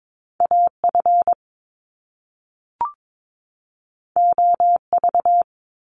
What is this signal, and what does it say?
Morse code, heard as a keyed tone, about 22 wpm, 700 Hz. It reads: AF OV
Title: Top Two Letter Combinations in Words 22wpm